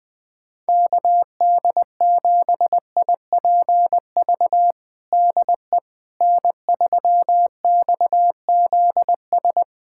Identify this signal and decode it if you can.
Morse code, heard as a keyed tone, about 21 wpm, 690 Hz. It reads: KD7IPV DE N3XZS
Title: USA Callsign DE Callsign 20wpm